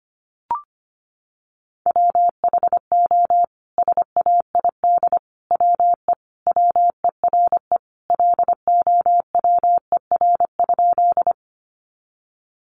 Morse code, heard as a keyed tone, about 25 wpm, 700 Hz. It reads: WHO SAID WE WERE LOWER?
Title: Sentences from Top 700 Words 25wpm